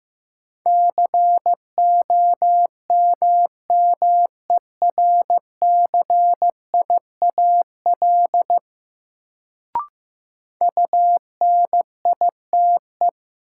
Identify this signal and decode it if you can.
Morse code, heard as a keyed tone, about 15 wpm, 700 Hz. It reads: COMMERCIAL UNITE
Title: Sentences from Top 1000 Words - Review Part 5 - New Word Drill - 15wpm